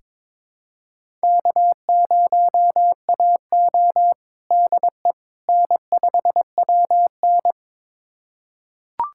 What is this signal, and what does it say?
Morse code, heard as a keyed tone, about 23 wpm, 700 Hz. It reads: K0AO DE N5WN
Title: USA Callsign DE Callsign - 4 Characters - 22wpm